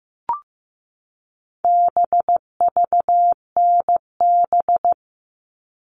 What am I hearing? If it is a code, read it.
Morse code, heard as a keyed tone, about 15 wpm, 700 Hz. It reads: BVNB
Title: All Cut Numbers - 4-Digit 15wpm